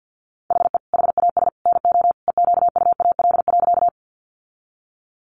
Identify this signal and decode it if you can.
Morse code, heard as a keyed tone, about 50 wpm, 700 Hz. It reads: HE HAS NO EQUAL.
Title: Sentences from Top 1400 Words 50wpm